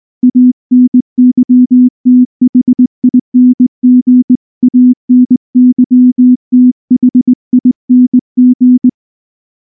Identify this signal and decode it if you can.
Morse code, heard as a keyed tone, about 19 wpm, 250 Hz. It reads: ANYTHING ANYTHING
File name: morse / morse-training-20WPM-4